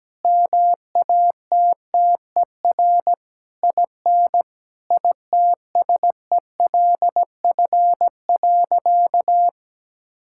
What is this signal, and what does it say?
Morse code, heard as a keyed tone, about 17 wpm, 700 Hz. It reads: MATTER IN ITSELF.
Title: Sentences from Top 1000 Words - Review Part 5 - 17wpm